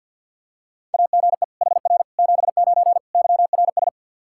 Morse code, heard as a keyed tone, about 48 wpm, 700 Hz. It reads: AGE HR 79 YRS
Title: Practice QSO - Exchange 3 - More Info - 50wpm